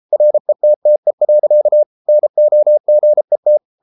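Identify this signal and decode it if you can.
Morse code, heard as a keyed tone, about 33 wpm, 600 Hz. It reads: RETTE. NOGET
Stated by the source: Morse code at 33 wpm, QRQ.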